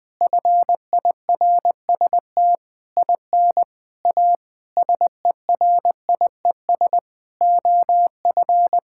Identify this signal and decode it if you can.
Morse code, heard as a keyed tone, about 20 wpm, 700 Hz. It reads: FIRST IN A SERIES OF
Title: Sentences with 8-Word Phrases - 20wpm